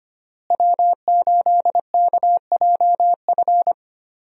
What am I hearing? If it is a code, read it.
Morse code, heard as a keyed tone, about 25 wpm, 700 Hz. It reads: W8KJF
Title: USA Call Signs - 5 Characters - Repeated 3x - 25wpm